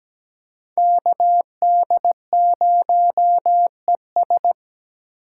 Morse code, heard as a keyed tone, about 17 wpm, 700 Hz. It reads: KD0ES
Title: USA Call Signs - 5 Characters - 4x Character Spacing - 17wpm